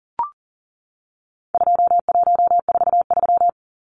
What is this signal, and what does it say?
Morse code, heard as a keyed tone, about 40 wpm, 700 Hz. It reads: 2143
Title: Numbers - 4-Digit 40wpm